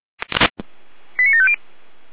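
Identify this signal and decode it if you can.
Five-tone selective call (ZVEI), heard as quick runs of five tones, tones of about 70 ms. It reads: 89750
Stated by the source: Five-tone ZVEI selective call received on an SDR.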